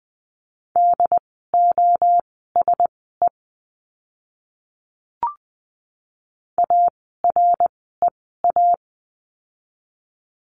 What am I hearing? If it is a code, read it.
Morse code, heard as a keyed tone, about 20 wpm, spaced as for 14 wpm, 700 Hz. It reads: DOSE AREA
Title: 068 Lesson 12f - Letter d - Words 20wpm